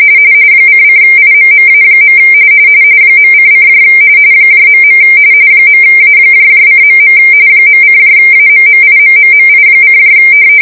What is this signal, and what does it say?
RTTY, about 45.45 baud. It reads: ::IMMA. .:: .. .A:.. .:AMMIMMMI: